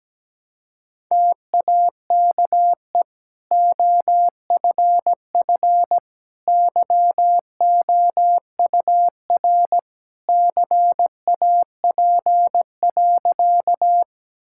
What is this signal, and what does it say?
Morse code, heard as a keyed tone, about 17 wpm, 700 Hz. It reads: TAKE OFF YOUR CAP.
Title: Sentences from Top 1600 Words 17wpm